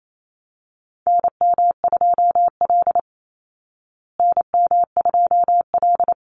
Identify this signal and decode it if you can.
Morse code, heard as a keyed tone, about 28 wpm, 700 Hz. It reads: NM2L NM2L
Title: Call Signs Repeated 3x - 28wpm